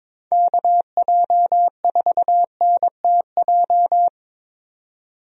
Morse code, heard as a keyed tone, about 22 wpm, 700 Hz. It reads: KJ4NTJ
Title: USA Call Signs - 6 Characters - 22wpm